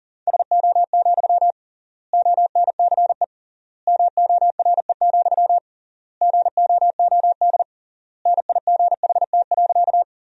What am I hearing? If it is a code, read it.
Morse code, heard as a keyed tone, about 40 wpm, 700 Hz. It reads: SO, ONCE MORE, GOOD NIGHT.
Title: 3x Word-Spacing - Sentences from Top 300 Words - 40wpm